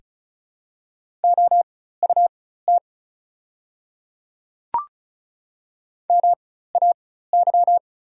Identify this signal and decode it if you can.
Morse code, heard as a keyed tone, about 36 wpm, spaced as for 16 wpm, 690 Hz. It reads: OUT MAY
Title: 4x Character Spacing - Top 100 Words - 35wpm